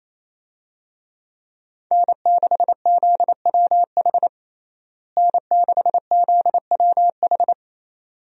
Morse code, heard as a keyed tone, about 27 wpm, 700 Hz. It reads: N6ZWH N6ZWH
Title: USA Call Signs - 5 Characters - Repeated 2x - 28wpm